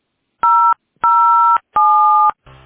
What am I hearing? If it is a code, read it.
DTMF, heard as telephone touch tones, tones of about 450 ms. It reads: 007